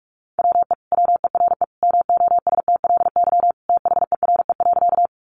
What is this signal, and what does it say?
Morse code, heard as a keyed tone, about 46 wpm, 700 Hz. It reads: WE WERE MOSTLY THERE.
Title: Sentences from Top 1300 Words 45wpm